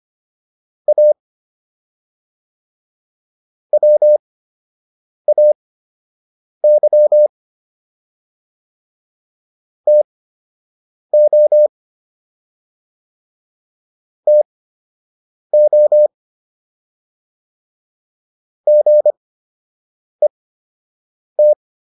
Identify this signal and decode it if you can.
Morse code, heard as a keyed tone, about 25 wpm, spaced as for 7 wpm, 600 Hz. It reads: A WAY TO TO GET